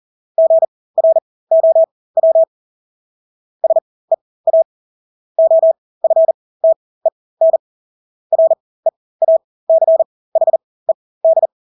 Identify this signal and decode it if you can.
Morse code, heard as a keyed tone, about 40 wpm, spaced as for 20 wpm, 650 Hz. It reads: GROW SEA OFTEN REACHED
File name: Engelske ord 40-20 wpm